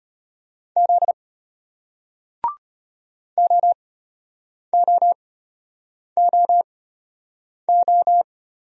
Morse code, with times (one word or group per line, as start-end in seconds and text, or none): 0.8-1.1 Z
3.4-8.2 OOOO